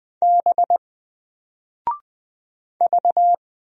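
Morse code, heard as a keyed tone, about 20 wpm, 700 Hz. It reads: B V
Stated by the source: Single letter and number Morse drill at 20 wpm, from N4LCD.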